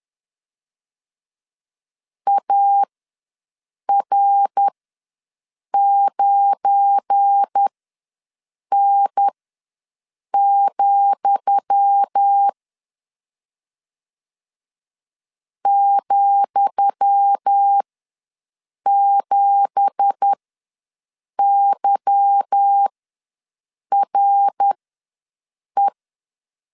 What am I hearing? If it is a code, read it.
Morse code, heard as a keyed tone, about 10 wpm, spaced as for 6 wpm, 800 Hz. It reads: AR9N, ,7YRE
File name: LEZIONI / 10 SIGNS / 10D COMA ws 10wpm 4314